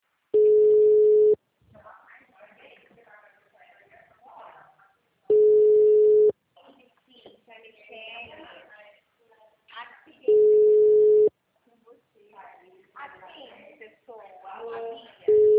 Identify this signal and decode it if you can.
Ringback tone (Europe, 425 Hz), with speech.